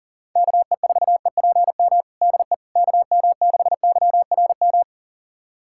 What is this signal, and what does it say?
Morse code, heard as a keyed tone, about 40 wpm, 700 Hz. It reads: KE4EPM DE KM6YRM
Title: USA Callsign DE Callsign - 6 Characters - 40wpm